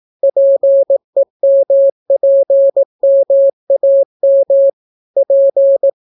Morse code, heard as a keyed tone, about 18 wpm, 550 Hz. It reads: PEMPMAM P